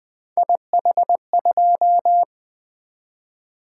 Morse code, heard as a keyed tone, about 24 wpm, 700 Hz. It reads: IH2
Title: Speed-Racing - Three Letter-Number Combinations - 20wpm